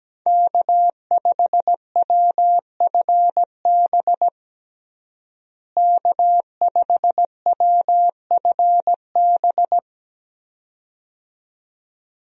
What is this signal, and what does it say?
Morse code, heard as a keyed tone, about 17 wpm, 700 Hz. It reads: K5WFB K5WFB
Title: USA Call Signs - 5 Characters - Repeated 2x - 17wpm